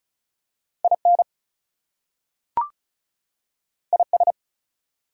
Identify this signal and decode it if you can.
Morse code, heard as a keyed tone, about 36 wpm, 700 Hz. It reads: IN IS
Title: Two Letter Words - 35wpm